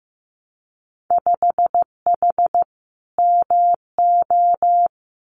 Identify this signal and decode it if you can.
Morse code, heard as a keyed tone, about 15 wpm, 700 Hz. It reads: 5H MO
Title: Winter Field Day Exchanges Encore 15wpm